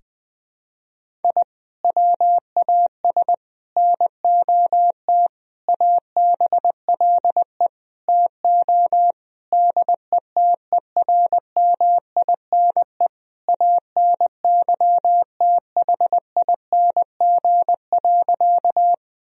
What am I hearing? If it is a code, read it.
Morse code, heard as a keyed tone, about 21 wpm, 700 Hz. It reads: I WAS NOT ABLE TO DETERMINE ANYTHING.
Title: Sentences from Top 1000 Words - Review Part 4 - 20wpm